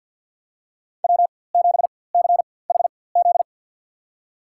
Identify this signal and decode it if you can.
Morse code, heard as a keyed tone, about 52 wpm, spaced as for 24 wpm, 700 Hz. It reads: W7CHZ
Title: USA Call Signs - 4x Character Spacing - 50wpm